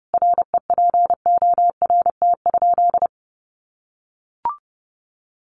Morse code, heard as a keyed tone, about 30 wpm, 700 Hz. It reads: REPORT?
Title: Sentences from Top 1200 Words 30wpm